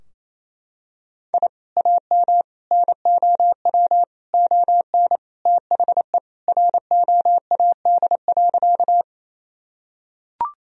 Morse code, heard as a keyed tone, about 28 wpm, 700 Hz. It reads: I AM NOW ON THE ROAD.
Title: Sentences from Top 1000 Words - Review Part 2 - 28wpm